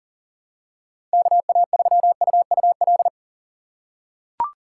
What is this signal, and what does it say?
Morse code, heard as a keyed tone, about 40 wpm, 700 Hz. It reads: KA3UUL
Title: USA Call Signs - 6x Character Spacing - 40wpm